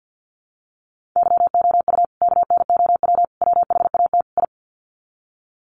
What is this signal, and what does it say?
Morse code, heard as a keyed tone, about 49 wpm, 700 Hz. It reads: YOU KNOW WHAT I
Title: Sets of 4 Words 50wpm